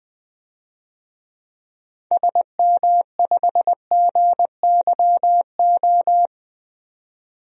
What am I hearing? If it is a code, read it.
Morse code, heard as a keyed tone, about 21 wpm, 700 Hz. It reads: SM5GYO
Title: DXCC-100 Call Signs - 6 Characters - 20wpm